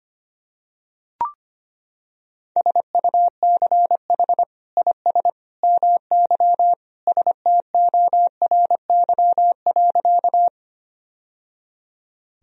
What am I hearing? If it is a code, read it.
Morse code, heard as a keyed tone, about 25 wpm, 700 Hz. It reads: SUCH IS MY STORY.
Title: Sentences from Top 1000 Words - Review Part 1 - 25wpm